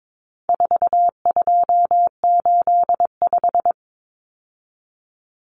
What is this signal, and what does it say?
Morse code, heard as a keyed tone, about 22 wpm, 700 Hz. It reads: 4285
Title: Common Cut Numbers - 4-Digit 22wpm